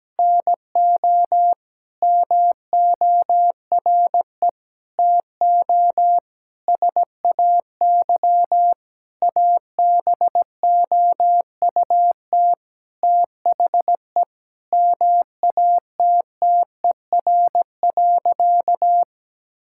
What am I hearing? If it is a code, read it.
Morse code, heard as a keyed tone, about 17 wpm, 700 Hz. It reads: NO MORE TO SAY ABOUT THE MATTER.
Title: Sentences from Top 1000 Words - Review Part 5 - 17wpm